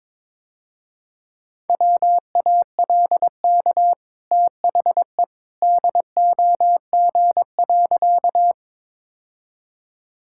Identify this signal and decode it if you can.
Morse code, heard as a keyed tone, about 22 wpm, 700 Hz. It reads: WALK THE DOG.
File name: practice / 22wpm / Easy Sentences with Hints - 22wpm